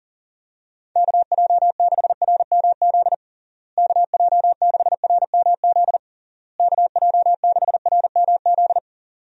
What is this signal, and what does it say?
Morse code, heard as a keyed tone, about 40 wpm, 700 Hz. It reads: KJ6RMZ KJ6RMZ KJ6RMZ
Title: USA Call Signs - 6 Characters - Repeated 3x - 40wpm